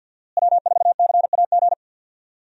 Morse code, heard as a keyed tone, about 47 wpm, 700 Hz. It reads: W4CAG
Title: USA Call Signs - 5 Characters - Repeated 2x - 50wpm